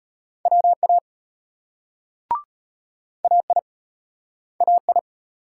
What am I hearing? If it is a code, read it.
Morse code, heard as a keyed tone, about 35 wpm, 700 Hz. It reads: WA AI AI